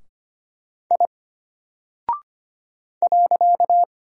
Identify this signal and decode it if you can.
Morse code, heard as a keyed tone, about 25 wpm, 700 Hz. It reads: I .
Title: Single Character (Letter:Number:Punctuation:Prosign) Alt 25wpm